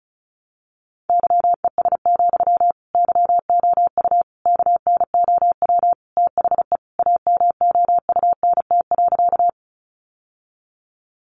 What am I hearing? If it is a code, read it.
Morse code, heard as a keyed tone, about 34 wpm, 700 Hz. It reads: YES, YOU KNOW THE AMOUNT.
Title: Sentences from Top 700 Words 35wpm